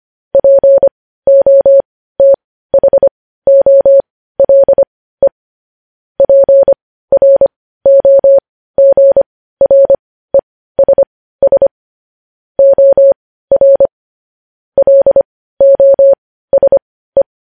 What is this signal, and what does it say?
Morse code, heard as a keyed tone, about 25 wpm, spaced as for 15 wpm, 550 Hz. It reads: POTHOLE PROGRESS OR LOSE
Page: news-headlines-in-morse-code-at-15-wpm